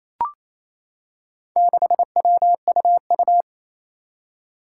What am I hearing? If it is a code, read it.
Morse code, heard as a keyed tone, about 28 wpm, 700 Hz. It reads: 6WUU
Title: Four Challenging Letter-Number Combinations 28wpm